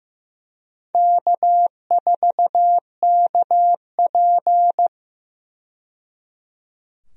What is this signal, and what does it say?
Morse code, heard as a keyed tone, about 16 wpm, 700 Hz. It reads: K4KP
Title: USA Call Signs - 4 Characters - 15wpm